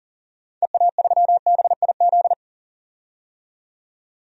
Morse code, heard as a keyed tone, about 40 wpm, 700 Hz. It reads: EA3BIZ
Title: DXCC-100 Call Signs - 6 Characters - 40wpm